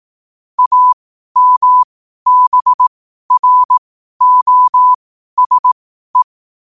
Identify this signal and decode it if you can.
Morse code, heard as a keyed tone, about 18 wpm, spaced as for 12 wpm, 1000 Hz. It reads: AMBROSE
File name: qotdfiles / 12wpm